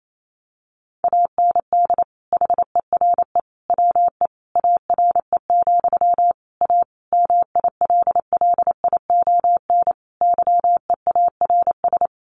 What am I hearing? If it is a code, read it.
Morse code, heard as a keyed tone, about 28 wpm, 700 Hz. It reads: AND HERE WE ARE, A MILLION YEARS
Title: A Sound Of Thunder 28wpm